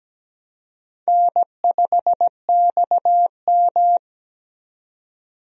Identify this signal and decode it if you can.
Morse code, heard as a keyed tone, about 17 wpm, 700 Hz. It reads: N5XM
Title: USA Call Signs - 4 Characters - 17wpm